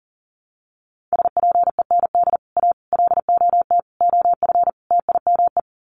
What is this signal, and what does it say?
Morse code, heard as a keyed tone, about 40 wpm, 700 Hz. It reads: SPEND A LOT OF TIME
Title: Sets of 5 Words 40wpm